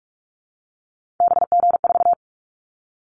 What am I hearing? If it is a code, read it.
Morse code, heard as a keyed tone, about 44 wpm, 700 Hz. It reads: BG4
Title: Callsign Prefixes for Top 100 DXCC - 45wpm